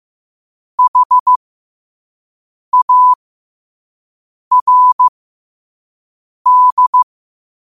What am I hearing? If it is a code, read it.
Morse code, heard as a keyed tone, about 15 wpm, spaced as for 5 wpm, 1000 Hz. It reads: HARD